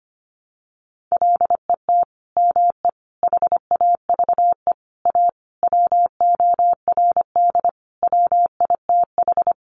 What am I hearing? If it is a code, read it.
Morse code, heard as a keyed tone, about 25 wpm, 700 Hz. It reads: LET ME HAVE A WORD WITH
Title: Sentences from Top 300 Words 25wpm